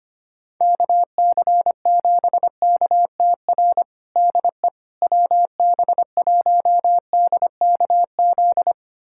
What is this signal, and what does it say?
Morse code, heard as a keyed tone, about 25 wpm, 700 Hz. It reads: KC7KTR DE WB1DKZ
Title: USA Callsign DE Callsign 25wpm